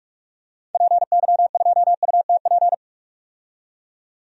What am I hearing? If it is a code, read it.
Morse code, heard as a keyed tone, about 44 wpm, 700 Hz. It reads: PY2UTP
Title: DXCC-100 Call Signs - 6 Characters - 45wpm